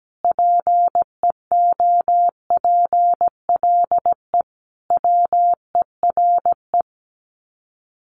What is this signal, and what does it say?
Morse code, heard as a keyed tone, about 17 wpm, 700 Hz. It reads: PEOPLE WERE
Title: Sets of 5 Words Encore2 17wpm